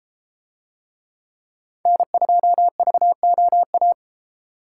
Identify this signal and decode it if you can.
Morse code, heard as a keyed tone, about 33 wpm, 700 Hz. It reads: N2VOA